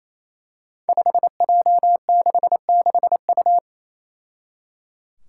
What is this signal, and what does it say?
Morse code, heard as a keyed tone, about 27 wpm, 700 Hz. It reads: 5J66U
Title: Five Challenging Letter-Number Combinations 28wpm